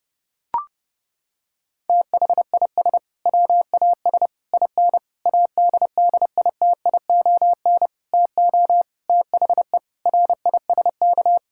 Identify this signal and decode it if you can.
Morse code, heard as a keyed tone, about 32 wpm, 700 Hz. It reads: THIS WAS IN ADDITION TO THE RISK
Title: Sentences from Top 1000 Words - Review Part 6 - 30wpm